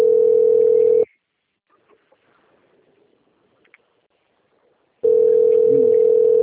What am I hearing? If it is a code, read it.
Ringback tone (North America), 440 + 480 Hz.
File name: rg-1600-unknown- - - .33141